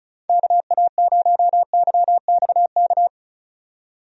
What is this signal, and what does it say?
Morse code, heard as a keyed tone, about 34 wpm, 700 Hz. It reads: KA0YXK
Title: USA Call Signs - 6 Characters - 35wpm